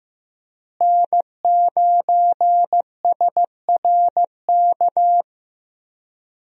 Morse code, heard as a keyed tone, about 15 wpm, 700 Hz. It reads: N9SRK
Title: USA Call Signs - 5 Characters - Repeated 2x - 15wpm